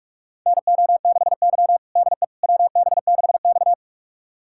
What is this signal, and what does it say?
Morse code, heard as a keyed tone, about 44 wpm, 690 Hz. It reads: NO6Y DE WB6X
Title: USA Callsign DE Callsign - 4 Characters - 45wpm